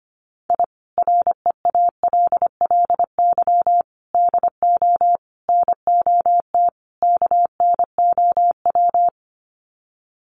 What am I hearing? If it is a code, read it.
Morse code, heard as a keyed tone, about 25 wpm, 700 Hz. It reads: I REALLY DO NOT KNOW
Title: Sets of 5 Words Encore 25wpm